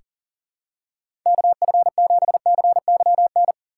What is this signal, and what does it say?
Morse code, heard as a keyed tone, about 40 wpm, 700 Hz. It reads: KF7CYN
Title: USA Call Signs - 6 Characters - 3x Character Spacing - 40wpm